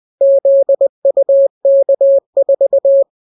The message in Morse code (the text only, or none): ZUK4